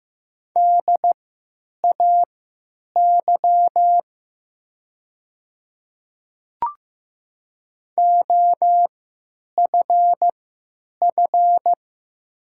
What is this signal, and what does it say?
Morse code, heard as a keyed tone, about 15 wpm, spaced as for 9 wpm, 700 Hz. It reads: DAY OFF
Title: 3x Character Spacing - Top 200 Words - 15wpm